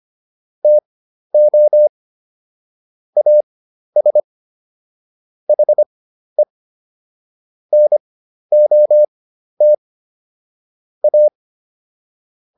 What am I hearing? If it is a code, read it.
Morse code, heard as a keyed tone, about 25 wpm, spaced as for 12 wpm, 600 Hz. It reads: TO AS HE NOT A